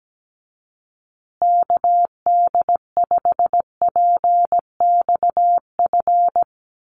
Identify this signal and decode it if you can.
Morse code, heard as a keyed tone, about 17 wpm, 700 Hz. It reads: KD5PXF
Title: USA Call Signs Repeated 3x 17wpm